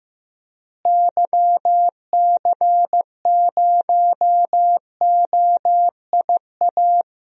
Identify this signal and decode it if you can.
Morse code, heard as a keyed tone, about 16 wpm, 700 Hz. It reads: YC0OIA
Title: DXCC-100 Call Signs - 6 Characters - Repeated 2x - 15wpm